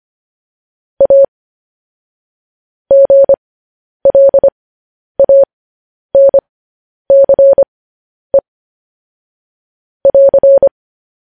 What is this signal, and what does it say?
Morse code, heard as a keyed tone, about 25 wpm, spaced as for 10 wpm, 550 Hz. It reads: A GLANCE <AR>
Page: news headlines in morse code at 10 wpm